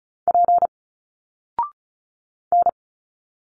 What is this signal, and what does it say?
Morse code, heard as a keyed tone, about 34 wpm, 700 Hz. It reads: P N
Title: Single Letters 35wpm